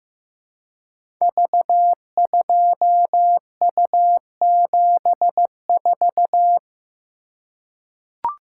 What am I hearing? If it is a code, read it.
Morse code, heard as a keyed tone, about 15 wpm, 700 Hz. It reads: V2U74
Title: Code Groups - 5 Letter:Numbers - 15wpm